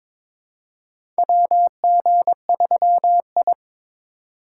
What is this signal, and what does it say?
Morse code, heard as a keyed tone, about 22 wpm, 700 Hz. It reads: WG3I
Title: USA Call Signs - 4 Characters - 2x Character Spacing - 22wpm